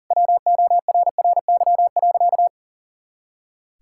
Morse code, heard as a keyed tone, about 40 wpm, 700 Hz. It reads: WORRY.
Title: Pangram Challenge v2 - 40wpm